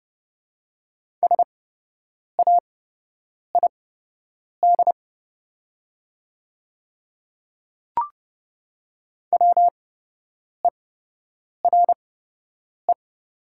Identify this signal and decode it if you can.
Morse code, heard as a keyed tone, about 30 wpm, spaced as for 8 wpm, 700 Hz. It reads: SAID WERE
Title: 8x Character Spacing - Top 100 Words - 30wpm